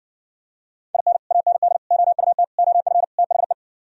Morse code, heard as a keyed tone, about 68 wpm, 700 Hz. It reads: IN AND OUT OF THE